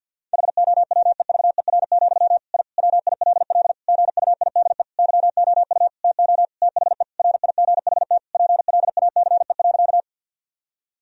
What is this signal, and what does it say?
Morse code, heard as a keyed tone, about 50 wpm, 700 Hz. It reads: HOWEVER, I WILL GUIDE YOU TO THE RIGHT PLACE.
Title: Sentences from Top 1000 Words - Review Part 3 - 50wpm